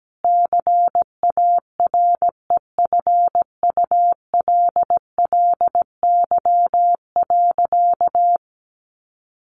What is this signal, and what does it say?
Morse code, heard as a keyed tone, about 17 wpm, 700 Hz. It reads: CAREFULLY.
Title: Sentences from Top 1300 Words 17wpm